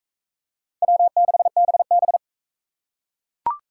Morse code, heard as a keyed tone, about 41 wpm, 700 Hz. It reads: W6BB